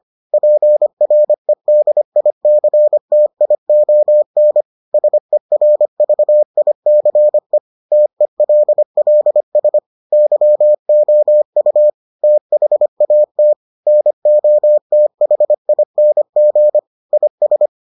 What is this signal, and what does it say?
Morse code, heard as a keyed tone, about 25 wpm, 600 Hz. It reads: PREDICTION SERVICE TELLS YOU THAT NOTHING IS